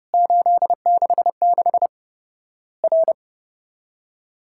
Morse code, heard as a keyed tone, about 30 wpm, 700 Hz. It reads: 866 R
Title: Worked All Europe DX Contest - QTC Exchange - 30wpm